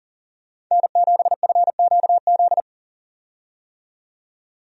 Morse code, heard as a keyed tone, about 40 wpm, 700 Hz. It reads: N7FQZ
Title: USA Call Signs - 5 Characters - 40wpm